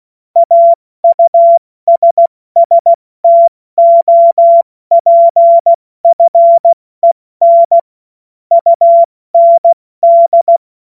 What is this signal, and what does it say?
Morse code, heard as a keyed tone, about 16 wpm, 680 Hz. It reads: AUSSTOPFEN UND
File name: Morsekurs / Witze / Witz 28. September 2021, 70 F80